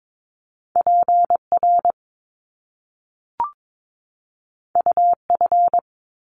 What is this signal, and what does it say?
Morse code, heard as a keyed tone, about 22 wpm, 700 Hz. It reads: PR UF
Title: Top Two Letter Combinations in Words 22wpm